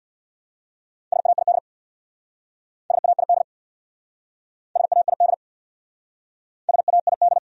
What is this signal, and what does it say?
Morse code, heard as a keyed tone, about 62 wpm, 700 Hz. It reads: SAID SAID SAID SAID
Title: Speed-Racing Top 100 Words 50wpm